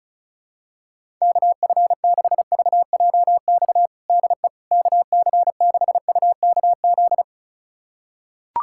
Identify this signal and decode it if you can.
Morse code, heard as a keyed tone, about 34 wpm, 700 Hz. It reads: KF6VJX DE KC6UKZ
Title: USA Callsign DE Callsign - 6 Characters - 35wpm